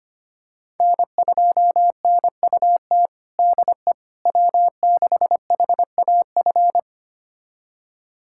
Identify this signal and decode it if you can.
Morse code, heard as a keyed tone, about 25 wpm, 700 Hz. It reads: N2NUT DE W6HAF
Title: USA Callsign DE Callsign - 5 Characters - 25wpm